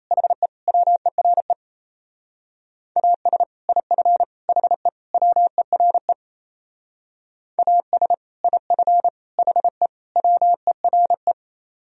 Morse code, heard as a keyed tone, about 32 wpm, 700 Hz. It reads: HE WERE AS IF HE WERE AS IF HE WERE
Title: Speed-Racing - Sets of 4 Words 28wpm